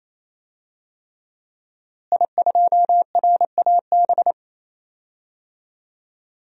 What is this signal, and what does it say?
Morse code, heard as a keyed tone, about 28 wpm, 700 Hz. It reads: I2RAB
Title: DXCC-100 Call Signs - 5 Characters - Repeated 3x - 28wpm